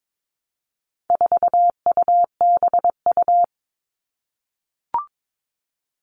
Morse code, heard as a keyed tone, about 22 wpm, 700 Hz. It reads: 4UBU
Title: All Cut Numbers - 4-Digit 22wpm